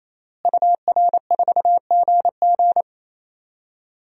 Morse code, heard as a keyed tone, about 28 wpm, 700 Hz. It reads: UR4GG
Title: DXCC-100 Call Signs - 5 Characters - Repeated 3x - 28wpm